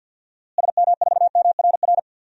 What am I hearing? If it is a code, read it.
Morse code, heard as a keyed tone, about 48 wpm, 700 Hz. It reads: SM4MRR
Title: DXCC-100 Call Signs - 6 Characters - Repeated 2x - 50wpm